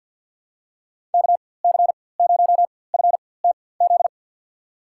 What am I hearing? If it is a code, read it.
Morse code, heard as a keyed tone, about 52 wpm, spaced as for 23 wpm, 700 Hz. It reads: KC0FTZ
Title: USA Call Signs - 6 Characters - 4x Character Spacing - 50wpm